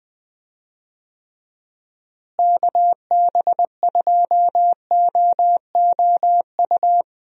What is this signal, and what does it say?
Morse code, heard as a keyed tone, about 21 wpm, 700 Hz. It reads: KB2OOU
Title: USA Call Signs - 6 Characters - Repeated 2x - 20wpm